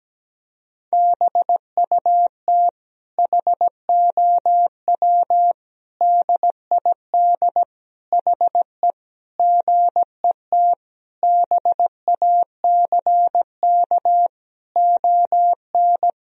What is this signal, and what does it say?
Morse code, heard as a keyed tone, about 17 wpm, 700 Hz. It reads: BUT HOW DID HE GET BACK ON
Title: Sentences from Top 1000 Words - Review Part 4 - 17wpm